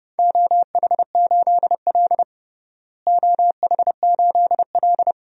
Morse code, heard as a keyed tone, about 32 wpm, 700 Hz. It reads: OH8L OH8L
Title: DXCC-100 Call Signs - 4 Characters - Repeated 2x - 30wpm